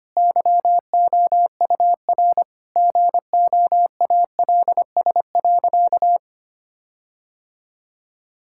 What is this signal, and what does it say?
Morse code, heard as a keyed tone, about 25 wpm, 700 Hz. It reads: YOUR GOALS.
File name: practice / 25wpm / Difficult Sentences with Hints - 25wpm